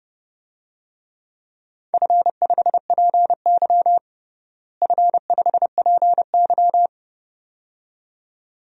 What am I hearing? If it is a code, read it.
Morse code, heard as a keyed tone, about 31 wpm, 700 Hz. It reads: F5PY F5PY
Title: DXCC-100 Call Signs - 4 Characters - Repeated 2x - 30wpm